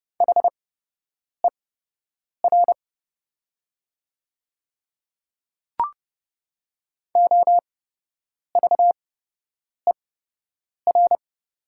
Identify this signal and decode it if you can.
Morse code, heard as a keyed tone, about 30 wpm, spaced as for 8 wpm, 700 Hz. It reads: HER OVER
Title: 8x Character Spacing - Top 300 Words - 30wpm